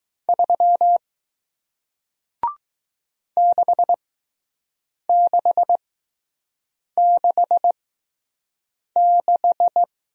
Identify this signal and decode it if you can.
Morse code, heard as a keyed tone, about 20 wpm, spaced as for 6 wpm, 700 Hz. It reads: 3 6666